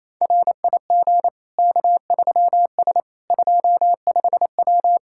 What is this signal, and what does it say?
Morse code, heard as a keyed tone, about 28 wpm, 700 Hz. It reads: RIG K3S 25W
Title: K1USN SST QSO Practice 28wpm